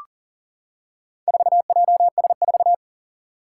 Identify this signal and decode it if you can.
Morse code, heard as a keyed tone, about 40 wpm, 690 Hz. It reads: 4JS4
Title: Four Challenging Letter-Number Combinations 40wpm